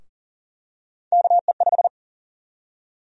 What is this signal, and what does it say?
Morse code, heard as a keyed tone, about 40 wpm, 700 Hz. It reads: KE5
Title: USA Call Sign Prefixes - 40wpm